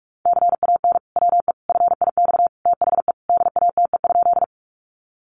Morse code, heard as a keyed tone, about 44 wpm, 700 Hz. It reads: CAN WE FIX THE DATE?
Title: Sentences from Top 1400 Words 45wpm